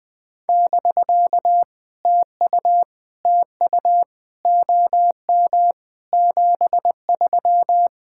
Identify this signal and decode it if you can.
Morse code, heard as a keyed tone, about 20 wpm, 700 Hz. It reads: <BK> TU TU OM 73
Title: POTA QSO Practice - 20wpm